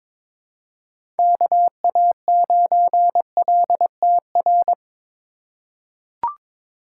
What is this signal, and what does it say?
Morse code, heard as a keyed tone, about 21 wpm, 700 Hz. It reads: KA9LTR
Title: USA Call Signs - 4x Character Spacing - 22wpm